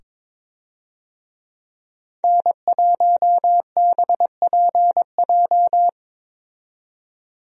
Morse code, heard as a keyed tone, about 22 wpm, 700 Hz. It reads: N1BPJ
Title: USA Call Signs - 5 Characters - 22wpm